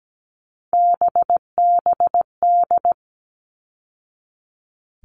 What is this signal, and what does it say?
Morse code, heard as a keyed tone, about 17 wpm, 700 Hz. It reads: BBD
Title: All Cut Numbers - 3-Digit 17wpm